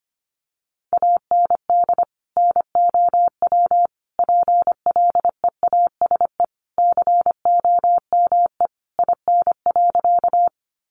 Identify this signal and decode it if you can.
Morse code, heard as a keyed tone, about 25 wpm, 700 Hz. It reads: AND NOW PLEASE COME IN.
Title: Sentences from Top 1500 Words 25wpm